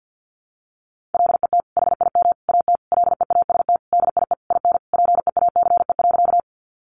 Morse code, heard as a keyed tone, about 48 wpm, 700 Hz. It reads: LET HIM AT LEAST DIE IN PEACE.
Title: Sentences from Top 1500 Words 50wpm